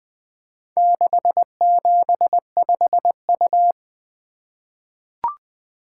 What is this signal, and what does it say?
Morse code, heard as a keyed tone, about 20 wpm, 700 Hz. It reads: 675U
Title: Four Letter-Number Combinations - 20wpm